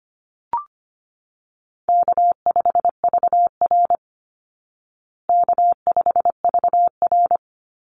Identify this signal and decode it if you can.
Morse code, heard as a keyed tone, about 25 wpm, 700 Hz. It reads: K5VR K5VR
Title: Easy Call Signs Repeated 3x - 25wpm